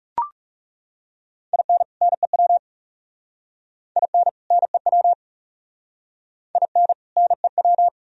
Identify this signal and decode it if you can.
Morse code, heard as a keyed tone, about 37 wpm, 700 Hz. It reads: IN NEW IN NEW IN NEW
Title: Speed-Racing - Sets of 2 Words 30wpm